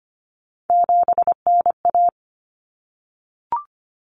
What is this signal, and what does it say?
Morse code, heard as a keyed tone, about 25 wpm, 700 Hz. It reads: 7NA
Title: Common Cut Numbers - 3-Digit 25wpm